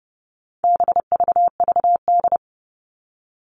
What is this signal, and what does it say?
Morse code, heard as a keyed tone, about 30 wpm, 700 Hz. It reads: BVVD
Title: All Cut Numbers - 4-Digit 30wpm